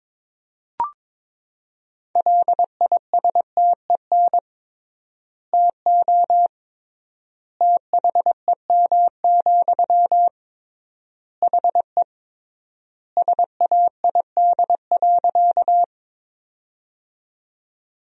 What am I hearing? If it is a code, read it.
Morse code, heard as a keyed tone, about 22 wpm, 700 Hz. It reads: LISTEN TO THEM, HE SAID.
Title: 3x Word-Spacing - Sentences from Top 400 Words - 22wpm